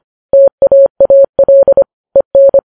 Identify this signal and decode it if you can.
Morse code, heard as a keyed tone, about 25 wpm, 550 Hz. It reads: TAAL EN
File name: Files Web tls / Morse on9fv / morse25kps